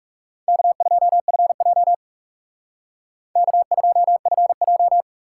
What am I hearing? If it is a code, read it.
Morse code, heard as a keyed tone, about 36 wpm, 700 Hz. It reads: K2FJ K2FJ
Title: Speed Racing - USA 1x2 Call Signs 30wpm